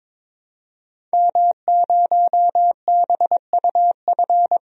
Morse code, heard as a keyed tone, about 22 wpm, 700 Hz. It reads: M0BUF
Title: DXCC-100 Call Signs - 5 Characters - 22wpm